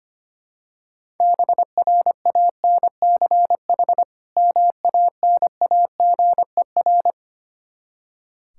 Morse code, heard as a keyed tone, about 25 wpm, 700 Hz. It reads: BRANCH MANAGER
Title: Practice QSO - Exchange 3 - More Info - 25wpm